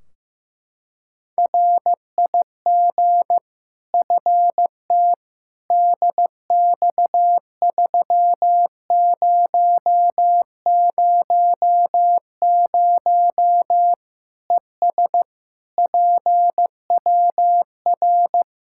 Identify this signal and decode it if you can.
Morse code, heard as a keyed tone, about 16 wpm, 700 Hz. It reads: RIG FT DX3000 ES PWR
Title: K1USN SST QSO Practice 15wpm